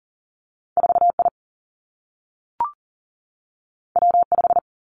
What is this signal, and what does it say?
Morse code, heard as a keyed tone, about 41 wpm, 700 Hz. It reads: 4I W5
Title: Two Challenging Letter-Number Combinations 40wpm